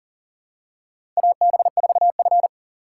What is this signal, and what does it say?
Morse code, heard as a keyed tone, about 40 wpm, 700 Hz. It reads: AB4F
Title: USA Call Signs - 4 Characters - 3x Character Spacing - 40wpm